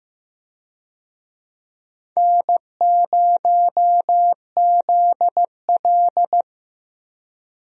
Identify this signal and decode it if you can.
Morse code, heard as a keyed tone, about 15 wpm, 700 Hz. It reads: N0ZL
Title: USA Call Signs - 4 Characters - Repeated 2x - 15wpm